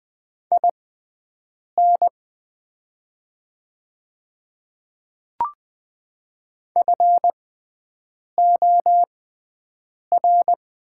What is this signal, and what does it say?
Morse code, heard as a keyed tone, about 20 wpm, spaced as for 7 wpm, 700 Hz. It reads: IN FOR
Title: 6x Character Spacing - Top 200 Words - 20wpm